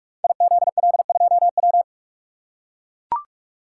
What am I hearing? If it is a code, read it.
Morse code, heard as a keyed tone, about 43 wpm, 700 Hz. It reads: IZL2W
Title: Code Groups - 5 Letter:Numbers - 45wpm